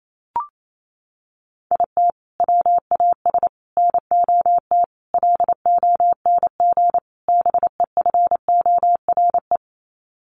Morse code, heard as a keyed tone, about 28 wpm, 700 Hz. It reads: IT WAS NOT LONG BEFORE
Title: Sets of 5 Words Encore2 28wpm